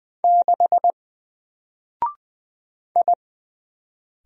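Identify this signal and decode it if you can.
Morse code, heard as a keyed tone, about 20 wpm, 700 Hz. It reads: 6 I
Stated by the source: Single letter and number Morse drill at 20 wpm, from N4LCD.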